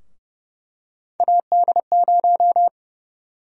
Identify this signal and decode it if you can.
Morse code, heard as a keyed tone, about 30 wpm, 700 Hz. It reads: AD0
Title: USA Call Sign Prefixes - Repeated 2x - 30wpm